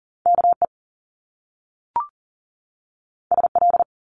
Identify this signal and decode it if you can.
Morse code, heard as a keyed tone, about 40 wpm, 700 Hz. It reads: KE SL
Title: Top Two Letter Combinations in Words 40wpm